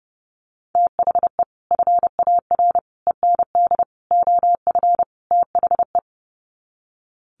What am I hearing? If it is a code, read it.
Morse code, heard as a keyed tone, about 29 wpm, 700 Hz. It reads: THE FAR END OF THE
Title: Sets of 5 Words 30wpm